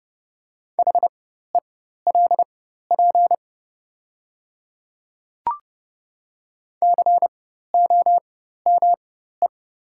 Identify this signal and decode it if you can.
Morse code, heard as a keyed tone, about 30 wpm, spaced as for 14 wpm, 700 Hz. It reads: HELP COME
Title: 4x Character Spacing - Top 200 Words - 30wpm